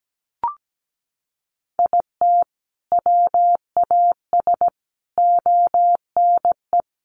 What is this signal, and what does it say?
Morse code, heard as a keyed tone, about 17 wpm, 700 Hz. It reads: IT WAS ONE
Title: Sets of 5 Words 17wpm